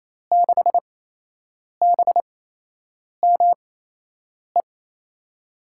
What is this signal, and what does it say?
Morse code, heard as a keyed tone, about 28 wpm, spaced as for 8 wpm, 700 Hz. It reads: 6BME
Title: USA Call Signs - 5 Characters - 8x Character Spacing - 28wpm